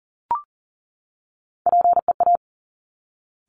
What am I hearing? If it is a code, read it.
Morse code, heard as a keyed tone, about 39 wpm, 700 Hz. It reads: PEA
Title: Top Three Letter Combinations in Words 40wpm